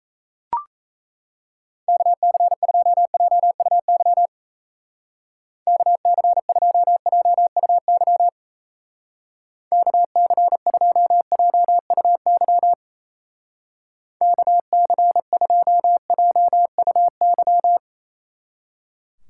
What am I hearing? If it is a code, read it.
Morse code, heard as a keyed tone, about 37 wpm, 700 Hz. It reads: KC2JUY KC2JUY KC2JUY KC2JUY